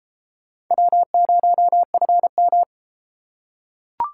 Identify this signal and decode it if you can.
Morse code, heard as a keyed tone, about 30 wpm, 700 Hz. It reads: W0FM